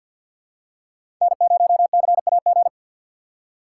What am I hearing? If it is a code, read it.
Morse code, heard as a keyed tone, about 53 wpm, 700 Hz. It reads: N0CAG
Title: USA Call Signs - 3x Character Spacing - 50wpm